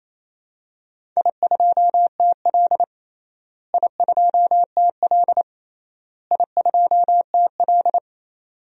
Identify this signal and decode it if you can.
Morse code, heard as a keyed tone, about 28 wpm, 700 Hz. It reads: I2TL I2TL I2TL
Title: DXCC-100 Call Signs - 4 Characters - Repeated 3x - 28wpm